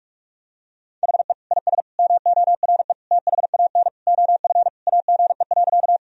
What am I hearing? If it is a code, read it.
Morse code, heard as a keyed tone, about 46 wpm, 700 Hz. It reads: HE IS MORE THAN OF AGE.
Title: Sentences from Top 1000 Words - Review Part 3 - 45wpm